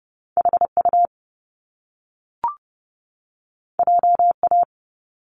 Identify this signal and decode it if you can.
Morse code, heard as a keyed tone, about 30 wpm, 700 Hz. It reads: HU JA
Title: Top Two Letter Combinations in Words 30wpm